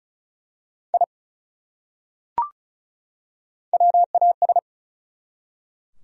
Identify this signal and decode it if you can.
Morse code, heard as a keyed tone, about 34 wpm, 700 Hz. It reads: I WAS
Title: Top 100 Words 35wpm